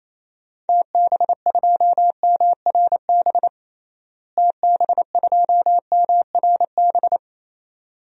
Morse code, heard as a keyed tone, about 28 wpm, 700 Hz. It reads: TB2MRB TB2MRB
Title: DXCC-100 Call Signs - 6 Characters - Repeated 2x - 28wpm